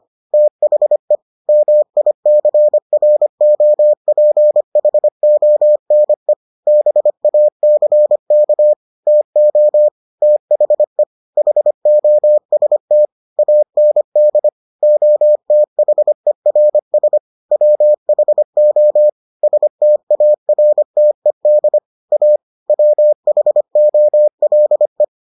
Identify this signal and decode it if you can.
Morse code, heard as a keyed tone, about 25 wpm, 600 Hz. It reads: THE MICROPHONE BACK TO THE HOST AND OTHERS WHO STARTED A WHOLE